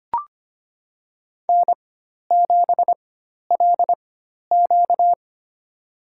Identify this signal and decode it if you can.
Morse code, heard as a keyed tone, about 25 wpm, spaced as for 12 wpm, 700 Hz. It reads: N7LQ
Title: USA Call Signs - 4 Characters - 4x Character Spacing - 25wpm